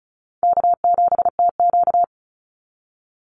Morse code, heard as a keyed tone, about 34 wpm, 700 Hz. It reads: K7TQ
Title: NA SOTA Chaser Callsigns 35wpm